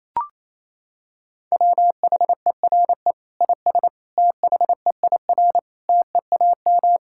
Morse code, heard as a keyed tone, about 27 wpm, 700 Hz. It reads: WHERE IS THEIR TEAM
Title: Sentences from Top 1600 Words 28wpm